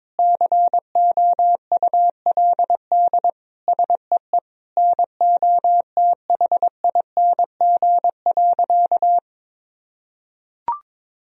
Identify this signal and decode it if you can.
Morse code, heard as a keyed tone, about 22 wpm, 700 Hz. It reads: COULD SEE NOTHING.
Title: Sentences from Top 1000 Words - Review Part 2 - 22wpm